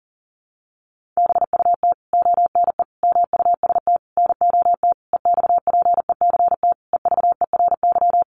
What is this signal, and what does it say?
Morse code, heard as a keyed tone, about 40 wpm, 700 Hz. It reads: BUT ONE MUST NOT EXPECT EVERY
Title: Sentences from Top 1100 Words 40wpm